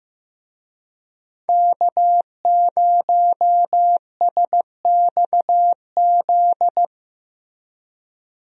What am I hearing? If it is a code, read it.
Morse code, heard as a keyed tone, about 16 wpm, 700 Hz. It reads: K0SXZ
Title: USA Call Signs - 5 Characters - Repeated 3x - 15wpm